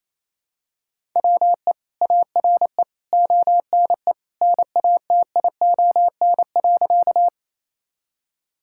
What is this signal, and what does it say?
Morse code, heard as a keyed tone, about 27 wpm, 700 Hz. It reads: WE ARE ONE NATION.
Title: Sentences from Top 1000 Words - Review Part 6 - 28wpm